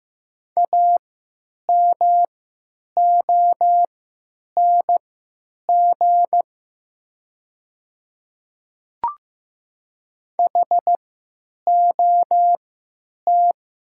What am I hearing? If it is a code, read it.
Morse code, heard as a keyed tone, about 15 wpm, spaced as for 8 wpm, 700 Hz. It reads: AMONG HOT
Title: 3x Character Spacing - Top 500 Words - 15wpm